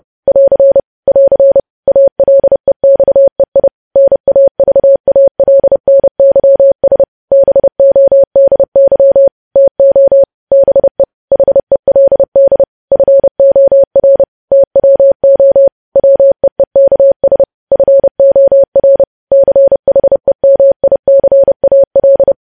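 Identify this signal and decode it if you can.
Morse code, heard as a keyed tone, about 30 wpm, 550 Hz. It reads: <AR> <AR> ALEXEI NAVALNYS BODY TO BE HELD FOR TWO WEEKS FOR CHEMICAL
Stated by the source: News headlines sent in Morse at 30.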